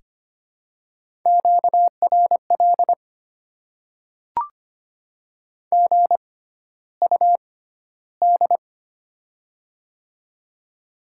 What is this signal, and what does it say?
Morse code, heard as a keyed tone, about 25 wpm, spaced as for 13 wpm, 700 Hz. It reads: QRL GUD
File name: Top 20 QSO Elements - 6x Character Spacing - 25wpm